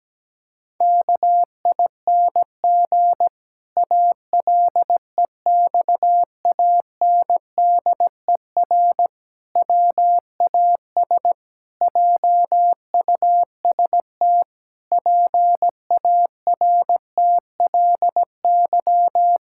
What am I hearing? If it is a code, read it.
Morse code, heard as a keyed tone, about 17 wpm, 700 Hz. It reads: KING ALEXANDER WAS JUST PARTLY
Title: Pangram Challenge v2 - 17wpm